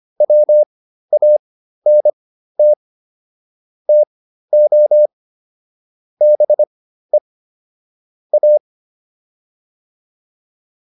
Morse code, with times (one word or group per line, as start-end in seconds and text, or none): 0.2-2.7 WANT
3.9-5.1 TO
6.2-7.2 BE
8.3-8.6 A